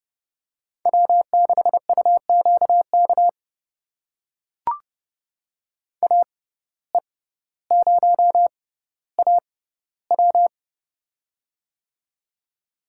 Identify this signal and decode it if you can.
Morse code, heard as a keyed tone, about 30 wpm, spaced as for 15 wpm, 700 Hz. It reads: W6UQK AE0AW
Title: USA Call Signs - 5 Characters - 6x Character Spacing - 30wpm